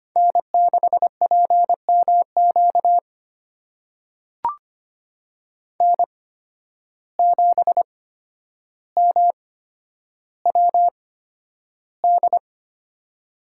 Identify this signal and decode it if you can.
Morse code, heard as a keyed tone, about 25 wpm, spaced as for 11 wpm, 700 Hz. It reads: N6PMQ N7MWD
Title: USA Call Signs - 5 Characters - 8x Character Spacing - 25wpm